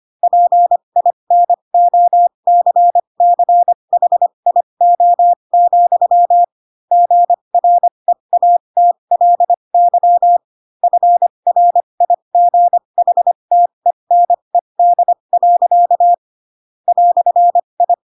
Morse code, heard as a keyed tone, about 25 wpm, spaced as for 22 wpm, 700 Hz. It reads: PINOCCHIO, GREATLY FRIGHTENED. "I
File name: Pinocchio / Pinocchio ch4 18 25 ik1hge